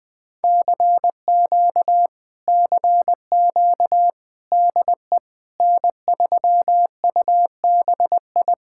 Morse code, heard as a keyed tone, about 20 wpm, 700 Hz. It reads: CQ CQ DE N3UBI